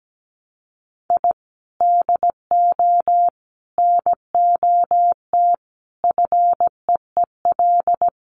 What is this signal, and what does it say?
Morse code, heard as a keyed tone, about 17 wpm, 700 Hz. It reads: I DO NOT FEEL
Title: Sets of 5 Words Encore 17wpm